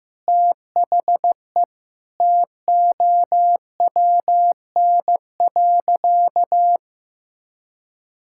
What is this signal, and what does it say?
Morse code, heard as a keyed tone, about 16 wpm, 700 Hz. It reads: THE TOWN.
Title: Sentences from Top 1000 Words - Review Part 5 - 15wpm